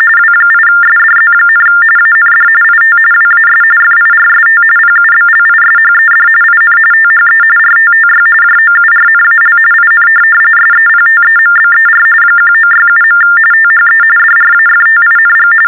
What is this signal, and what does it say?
TTY/TDD text telephone tones, about 45.45 baud. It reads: W8SDZ, AND IT WAS FIRST PUBLISHED IN THE RTTY JOURNAL IN SEPTEMBER 1967. THE DRAFTING
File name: teletype-RTTY-45